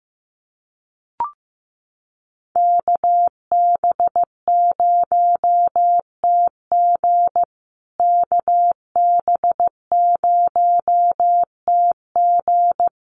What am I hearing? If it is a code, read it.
Morse code, heard as a keyed tone, about 15 wpm, 700 Hz. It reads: KB0TG KB0TG
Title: USA Call Signs Repeated 3x 15wpm